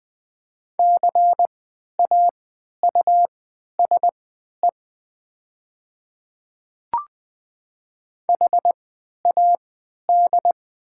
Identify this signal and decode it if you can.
Morse code, heard as a keyed tone, about 20 wpm, spaced as for 11 wpm, 690 Hz. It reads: CAUSE HAD
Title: 3x Character Spacing - Top 200 Words - 20wpm